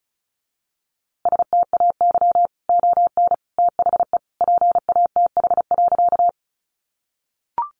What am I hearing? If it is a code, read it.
Morse code, heard as a keyed tone, about 34 wpm, 700 Hz. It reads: STAY ON THE PATH.
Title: A Sound Of Thunder 35wpm